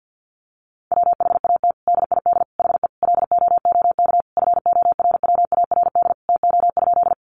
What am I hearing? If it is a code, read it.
Morse code, heard as a keyed tone, about 50 wpm, 700 Hz. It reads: WHAT DID HE LOOK FORWARD TO?
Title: Sentences from Top 1300 Words 50wpm